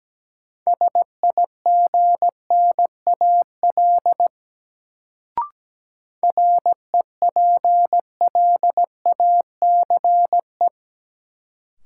Morse code, heard as a keyed tone, about 17 wpm, 700 Hz. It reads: SIGNAL REPLACE
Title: Sentences from Top 1400 Words - New Word Drill - 17wpm